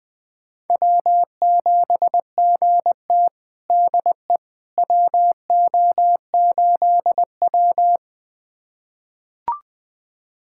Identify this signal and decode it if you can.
Morse code, heard as a keyed tone, about 21 wpm, 700 Hz. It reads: W7GT DE WO8W
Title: USA Callsign DE Callsign - 4 Characters - 20wpm